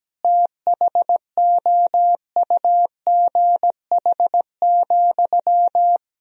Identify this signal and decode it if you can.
Morse code, heard as a keyed tone, about 17 wpm, 700 Hz. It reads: THOUGH,
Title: Sentences from Top 1000 Words - Review Part 2 - 17wpm